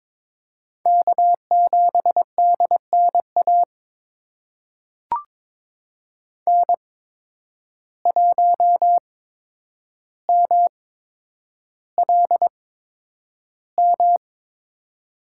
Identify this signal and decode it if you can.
Morse code, heard as a keyed tone, about 22 wpm, spaced as for 9 wpm, 700 Hz. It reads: K7DNA N1MLM
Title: USA Call Signs - 5 Characters - 8x Character Spacing - 22wpm